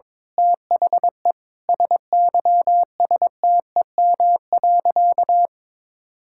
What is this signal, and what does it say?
Morse code, heard as a keyed tone, about 22 wpm, 700 Hz. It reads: THE SYSTEM.
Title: Sentences from Top 1000 Words - Review Part 6 - 22wpm